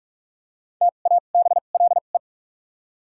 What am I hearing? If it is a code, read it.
Morse code, heard as a keyed tone, about 41 wpm, spaced as for 31 wpm, 700 Hz. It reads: TABLE
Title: 2x Character Spacing - Top 500 Words - 45wpm